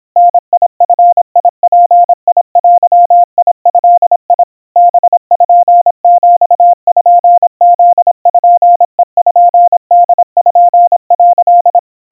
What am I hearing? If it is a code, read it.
Morse code (Japanese Wabun code), heard as a keyed tone, about 26 wpm, 700 Hz. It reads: ダヂヅデド パピプペポ」
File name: aiueo 90 cpm 700 Hz Crispy